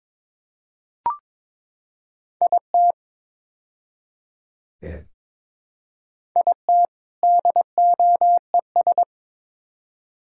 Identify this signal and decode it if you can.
Morse code, heard as a keyed tone, about 22 wpm, 700 Hz. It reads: IT IT DOES
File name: practice / 22wpm / Build A Phrase - 5-Word Phrases - 22wpm